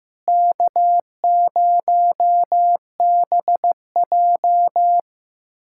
Morse code, heard as a keyed tone, about 15 wpm, 700 Hz. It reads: K0BJ
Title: USA Callsign DE Callsign - 4 Characters - 15wpm